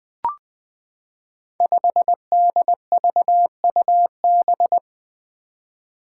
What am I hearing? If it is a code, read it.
Morse code, heard as a keyed tone, about 20 wpm, 700 Hz. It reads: 5DVUB
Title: Five Challenging Letter-Number Combinations 20wpm